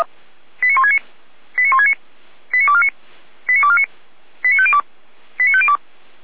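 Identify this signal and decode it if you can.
Five-tone selective call (ZVEI), heard as quick runs of five tones, tones of about 70 ms. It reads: 89169 89169 89249 89249 89502 89502
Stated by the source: ZVEI five-tone selective call received by SDR, on the feuerwehr-orlamuende.org fire brigade server.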